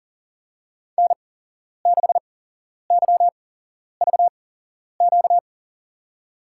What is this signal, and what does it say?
Morse code, heard as a keyed tone, about 40 wpm, spaced as for 11 wpm, 700 Hz. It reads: N6YVQ
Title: USA Call Signs - 5 Characters - 8x Character Spacing - 40wpm